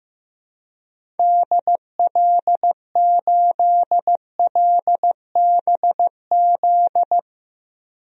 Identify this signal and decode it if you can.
Morse code, heard as a keyed tone, about 15 wpm, 700 Hz. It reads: DL8LBZ
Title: DXCC-100 Call Signs - 6 Characters - 15wpm